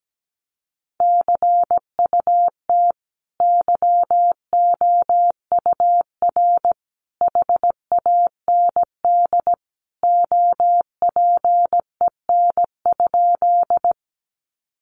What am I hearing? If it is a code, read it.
Morse code, heard as a keyed tone, about 17 wpm, 700 Hz. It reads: CUT YOUR HAND OPEN?
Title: Sentences from Top 300 Words 17wpm